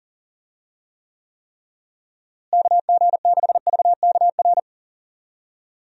Morse code, heard as a keyed tone, about 40 wpm, 700 Hz. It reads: KG6VKR
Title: USA Call Signs - 6 Characters - Repeated 3x - 40wpm